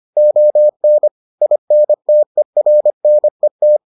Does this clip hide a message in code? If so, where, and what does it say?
Yes: Morse code from 0.2 to 3.8 s, heard as a keyed tone, about 25 wpm, 600 Hz. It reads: ON INTERNET